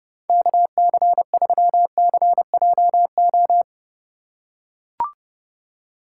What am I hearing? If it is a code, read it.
Morse code, heard as a keyed tone, about 30 wpm, 700 Hz. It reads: KC3CJO
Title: USA Call Signs - 2x Character Spacing - 30wpm